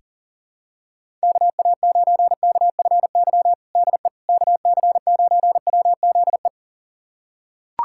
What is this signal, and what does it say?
Morse code, heard as a keyed tone, about 40 wpm, 700 Hz. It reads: KA9KFY DE KC9WZE
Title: USA Callsign DE Callsign - 6 Characters - 40wpm